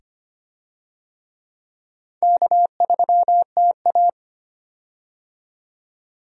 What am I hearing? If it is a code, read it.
Morse code, heard as a keyed tone, about 25 wpm, 700 Hz. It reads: K3TA
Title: USA Call Signs - 4 Characters - 25wpm